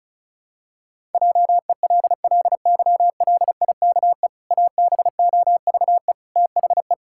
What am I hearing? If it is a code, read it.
Morse code, heard as a keyed tone, about 36 wpm, 700 Hz. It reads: JELLYLIKE ABOVE THE
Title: Pangram Challenge v2 - 35wpm